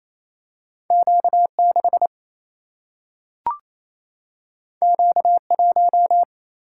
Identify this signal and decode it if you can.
Morse code, heard as a keyed tone, about 28 wpm, 700 Hz. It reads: Q 6 Q 1
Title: Single-Letter Single-Number 28wpm